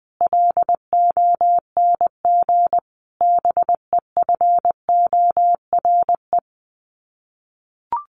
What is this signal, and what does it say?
Morse code, heard as a keyed tone, about 20 wpm, 700 Hz. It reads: LONG BEFORE
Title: Sets of 5 Words Encore2 20wpm